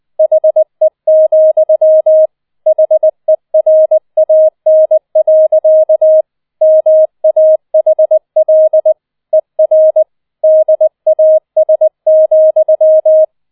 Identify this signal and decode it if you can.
Morse code, heard as a keyed tone, about 20 wpm, 630 Hz. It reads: HE, HERAN. MAHL ER DAS,